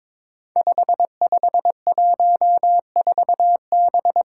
Morse code, heard as a keyed tone, about 23 wpm, 700 Hz. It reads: 5514B
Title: Five Challenging Letter-Number Combinations 22wpm